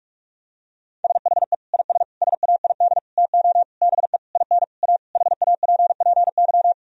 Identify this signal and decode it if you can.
Morse code, heard as a keyed tone, about 46 wpm, 700 Hz. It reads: SHE IS SAID TO BE IN A HAPPY
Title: Sentences from Top 1000 Words - Review Part 2 - 45wpm